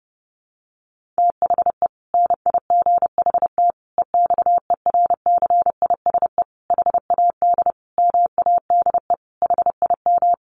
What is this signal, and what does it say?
Morse code, heard as a keyed tone, about 30 wpm, 700 Hz. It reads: THE NIGHT EXERCISE HAD MADE HIM
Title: Sentences from Top 1400 Words 30wpm